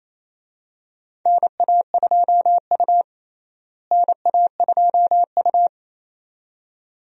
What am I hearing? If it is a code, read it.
Morse code, heard as a keyed tone, about 28 wpm, 700 Hz. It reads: NA2U NA2U
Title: USA Call Signs - 4 Characters - Repeated 2x - 28wpm